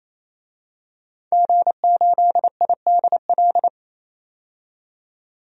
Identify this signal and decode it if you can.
Morse code, heard as a keyed tone, about 28 wpm, 700 Hz. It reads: G8IDL
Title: DXCC-100 Call Signs - 5 Characters - 28wpm